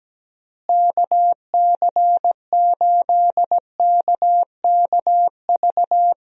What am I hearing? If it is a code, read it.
Morse code, heard as a keyed tone, about 17 wpm, 700 Hz. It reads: KC8KKV
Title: USA Call Signs - 6 Characters - Repeated 2x - 17wpm